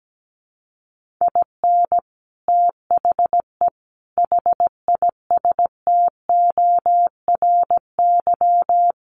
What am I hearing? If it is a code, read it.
Morse code, heard as a keyed tone, about 17 wpm, 700 Hz. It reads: IN THE HISTORY
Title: Sets of 5 Words 17wpm